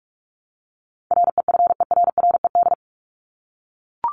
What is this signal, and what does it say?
Morse code, heard as a keyed tone, about 44 wpm, 700 Hz. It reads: REFERRED
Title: ED Suffix - Top 100 Words with Base Words 45wpm